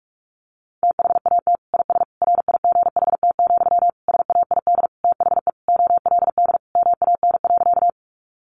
Morse code, heard as a keyed tone, about 44 wpm, 700 Hz. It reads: THAT IS RIGHT, SAID THE OLD MAN.
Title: Sentences from Top 200 Words 45wpm